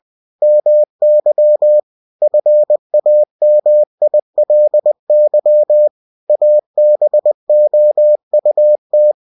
Morse code, heard as a keyed tone, about 20 wpm, 600 Hz. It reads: MY FAMILY ABOUT
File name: cw-books / winnie-the-pooh / winnie-20wpm-page-0060